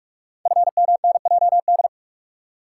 Morse code, heard as a keyed tone, about 44 wpm, 690 Hz. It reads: FMNJD
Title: Code Groups - 5 Letters - 45wpm